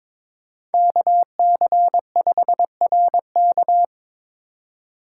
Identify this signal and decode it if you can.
Morse code, heard as a keyed tone, about 22 wpm, 700 Hz. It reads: KC5RK
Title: USA Call Signs - 5 Characters - 2x Character Spacing - 22wpm